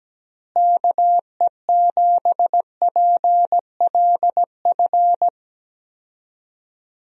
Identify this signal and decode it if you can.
Morse code, heard as a keyed tone, about 17 wpm, 700 Hz. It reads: KE7PLF
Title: USA Call Signs - 6 Characters - Repeated 3x - 17wpm